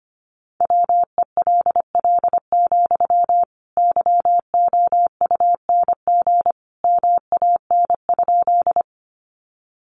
Morse code, heard as a keyed tone, about 25 wpm, 700 Hz. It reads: WELL, YOUNG MAN?
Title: Sentences from Top 400 Words 25wpm